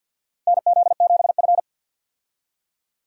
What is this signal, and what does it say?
Morse code, heard as a keyed tone, about 50 wpm, 700 Hz. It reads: NZ7F
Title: USA Call Signs - 4 Characters - 50wpm